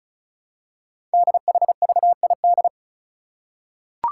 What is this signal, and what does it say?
Morse code, heard as a keyed tone, about 36 wpm, 690 Hz. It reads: DHVID
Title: Code Groups - 5 Letters - 35wpm